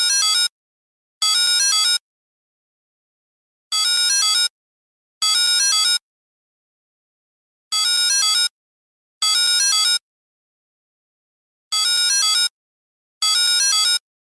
There is no code.